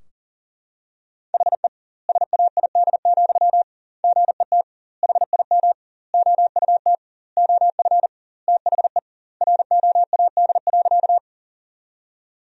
Morse code, heard as a keyed tone, about 40 wpm, 700 Hz. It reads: HE SAID, GET HIM OUT OF THE ROAD.
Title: 2x Word-Spacing - Sentences from Top 400 Words - 40wpm